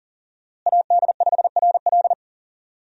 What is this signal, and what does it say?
Morse code, heard as a keyed tone, about 40 wpm, 700 Hz. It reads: AD5RL
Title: USA Call Signs - 5 Characters - Repeated 3x - 40wpm